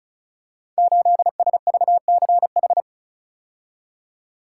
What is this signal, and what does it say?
Morse code, heard as a keyed tone, about 36 wpm, 700 Hz. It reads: 8SVCH
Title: Code Groups - 5 Letter:Numbers - 35wpm